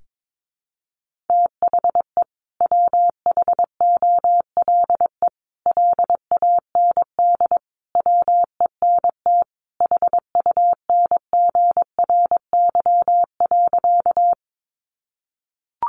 Morse code, heard as a keyed tone, about 22 wpm, 700 Hz. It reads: THE WHOLE LAND WENT HUNGRY.
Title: Sentences from Top 1400 Words 22wpm